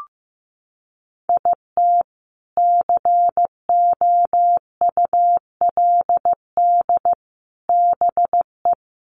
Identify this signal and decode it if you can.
Morse code, heard as a keyed tone, about 15 wpm, 700 Hz. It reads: IT COULD BE
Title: Sentences from Top 800 Words 15wpm